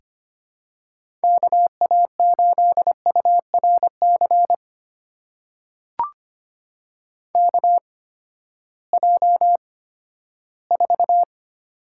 Morse code, heard as a keyed tone, about 25 wpm, spaced as for 14 wpm, 700 Hz. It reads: KA8URC KJ4
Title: USA Call Signs - 6 Characters - 8x Character Spacing - 25wpm